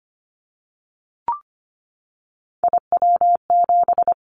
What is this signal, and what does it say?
Morse code, heard as a keyed tone, about 25 wpm, 690 Hz. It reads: IW7
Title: Callsign Prefixes for Top 50 DXCC - 25wpm